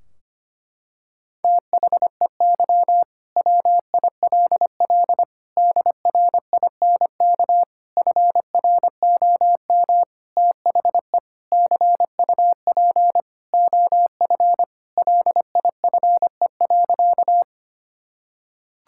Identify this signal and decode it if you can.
Morse code, heard as a keyed tone, about 25 wpm, 700 Hz. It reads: THEY WILL DRINK FROM THE CUP OF LIFE.
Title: Sentences from Top 1600 Words 25wpm